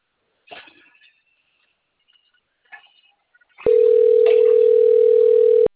Ringback tone (North America), 440 + 480 Hz.